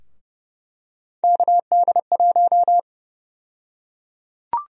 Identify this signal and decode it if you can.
Morse code, heard as a keyed tone, about 31 wpm, 700 Hz. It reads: KD1
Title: USA Call Sign Prefixes - 30wpm